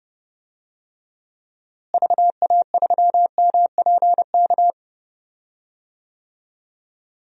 Morse code, heard as a keyed tone, about 30 wpm, 700 Hz. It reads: VA3MPK
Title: DXCC-100 Call Signs - 6 Characters - Repeated 2x - 30wpm